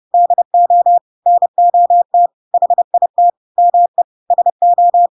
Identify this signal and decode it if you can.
Morse code, heard as a keyed tone, about 30 wpm, 700 Hz. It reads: DO NOT HIT ME SO
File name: Pinocchio / Pinocchio ch1 30 30 ik1hge